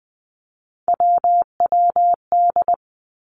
Morse code, heard as a keyed tone, about 20 wpm, 700 Hz. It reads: WWD
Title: Three Challenging Letter-Number Combinations 20wpm